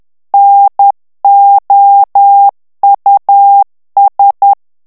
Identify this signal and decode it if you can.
Morse code, heard as a keyed tone, about 10 wpm, 800 Hz. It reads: NOUS
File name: morse mp3 / FRANCAIS TEXTE CLAIR 2 VITESSE 10 MOTS MINUTES